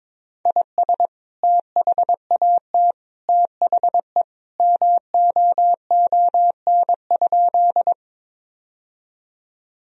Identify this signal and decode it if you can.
Morse code, heard as a keyed tone, about 22 wpm, 700 Hz. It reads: IS THAT THE MOON?
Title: Sentences from Top 1000 Words - Review Part 3 - 22wpm